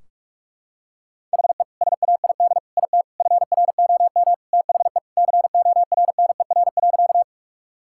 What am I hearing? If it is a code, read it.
Morse code, heard as a keyed tone, about 41 wpm, 700 Hz. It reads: HE SAID IT FROM THE CORNER.
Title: Sentences from Top 1600 Words 45wpm